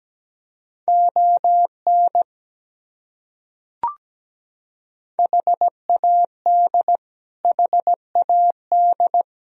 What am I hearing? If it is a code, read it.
Morse code, heard as a keyed tone, about 17 wpm, 700 Hz. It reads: ON HAD HAD
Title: Top 100 Words Repeated 17wpm